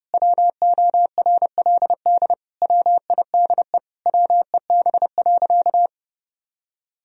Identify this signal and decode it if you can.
Morse code, heard as a keyed tone, about 32 wpm, 700 Hz. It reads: WORLD WIDE WEB.
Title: Sentences from Top 1000 Words - Review Part 4 - 30wpm